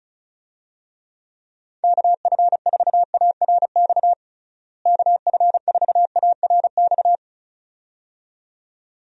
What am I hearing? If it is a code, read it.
Morse code, heard as a keyed tone, about 36 wpm, 700 Hz. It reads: KF4ARX KF4ARX
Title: USA Call Signs - 6 Characters - Repeated 2x - 35wpm